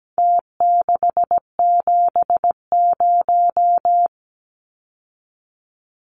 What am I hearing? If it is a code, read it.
Morse code, heard as a keyed tone, about 17 wpm, 700 Hz. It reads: T670
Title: Numbers - Contest Style - 4-Digit 17wpm